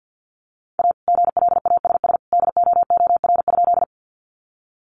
Morse code, heard as a keyed tone, about 50 wpm, 700 Hz. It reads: A GLASS DOOR?
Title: Sentences from Top 1300 Words 50wpm